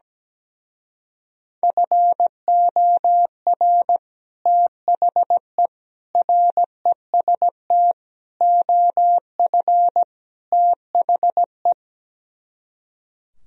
Morse code, heard as a keyed tone, about 17 wpm, 700 Hz. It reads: FOR THE REST OF THE
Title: Speed-Racing - Sets of 5 Words 17wpm